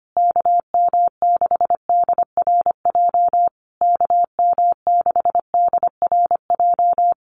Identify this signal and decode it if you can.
Morse code, heard as a keyed tone, about 25 wpm, 700 Hz. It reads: KM6DRJ KM6DRJ
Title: USA Call Signs Repeated 3x 25wpm